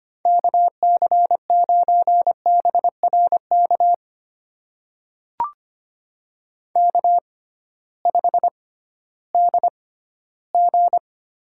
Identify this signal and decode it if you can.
Morse code, heard as a keyed tone, about 25 wpm, spaced as for 15 wpm, 700 Hz. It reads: KC9BRK K5DG
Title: USA Call Signs - 6x Character Spacing - 25wpm